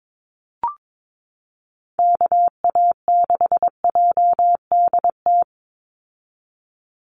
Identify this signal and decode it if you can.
Morse code, heard as a keyed tone, about 22 wpm, 700 Hz. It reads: KA6JDT
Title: USA Call Signs 22wpm